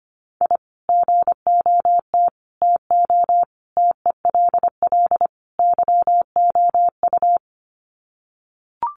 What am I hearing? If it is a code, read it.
Morse code, heard as a keyed tone, about 25 wpm, 700 Hz. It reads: I GOT TO TELL YOU
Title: Sets of 5 Words Encore2 25wpm